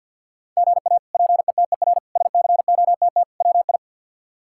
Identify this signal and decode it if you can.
Morse code, heard as a keyed tone, about 52 wpm, 700 Hz. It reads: GA PETER SCOTT WI
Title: K1USN SST QSO Practice - 50wpm